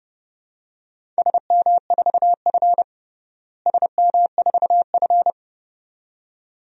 Morse code, heard as a keyed tone, about 32 wpm, 700 Hz. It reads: SM4F SM4F
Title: DXCC-100 Call Signs - 4 Characters - Repeated 2x - 30wpm